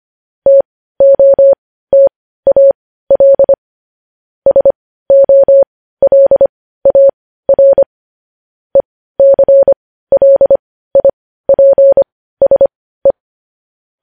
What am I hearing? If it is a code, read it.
Morse code, heard as a keyed tone, about 25 wpm, spaced as for 15 wpm, 550 Hz. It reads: TOTAL SOLAR ECLIPSE